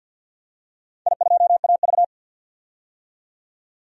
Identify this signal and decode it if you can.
Morse code, heard as a keyed tone, about 48 wpm, 700 Hz. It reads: I2AV
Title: DXCC-100 Call Signs - 4 Characters - 50wpm